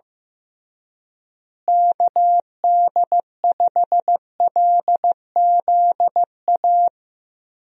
Morse code, heard as a keyed tone, about 15 wpm, 700 Hz. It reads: KD5LZA
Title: USA Call Signs - 6 Characters - Repeated 3x - 15wpm